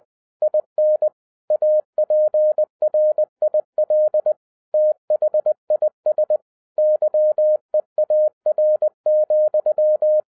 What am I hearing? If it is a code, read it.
Morse code, heard as a keyed tone, about 20 wpm, 610 Hz. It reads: IN APRIL THIS YEAR,
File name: 2012-01-mp3 / n16-THROUGH EIGHT MONTHS-20wpm